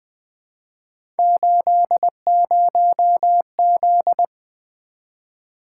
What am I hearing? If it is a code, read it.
Morse code, heard as a keyed tone, about 20 wpm, 700 Hz. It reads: 80Z
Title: Three Letter-Number Combinations - 20wpm